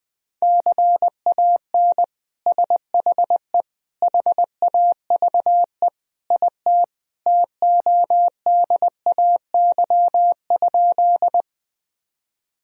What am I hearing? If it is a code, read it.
Morse code, heard as a keyed tone, about 21 wpm, 700 Hz. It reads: CAN SHE HAVE IT TODAY?
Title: Sentences from Top 1000 Words - Review Part 4 - 20wpm